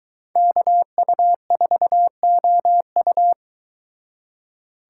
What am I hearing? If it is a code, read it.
Morse code, heard as a keyed tone, about 22 wpm, 700 Hz. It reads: KU4OU